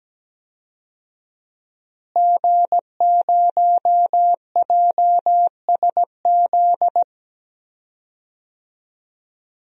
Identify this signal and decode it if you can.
Morse code, heard as a keyed tone, about 17 wpm, 700 Hz. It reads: G0JSZ
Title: DXCC-100 Call Signs - 5 Characters - Repeated 2x - 17wpm